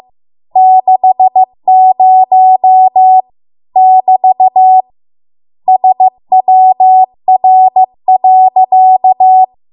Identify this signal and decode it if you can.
Morse code, heard as a keyed tone, about 15 wpm, 750 Hz. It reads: 60 <BT> SWR.